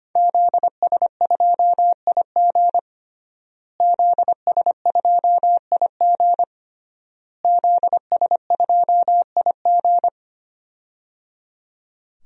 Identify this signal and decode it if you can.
Morse code, heard as a keyed tone, about 25 wpm, 700 Hz. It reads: ZS2IG ZS2IG ZS2IG
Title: DXCC-100 Call Signs - 5 Characters - Repeated 3x - 25wpm